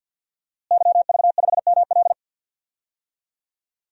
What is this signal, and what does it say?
Morse code, heard as a keyed tone, about 49 wpm, 700 Hz. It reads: YV5ML
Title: DXCC-100 Call Signs - 5 Characters - Repeated 3x - 50wpm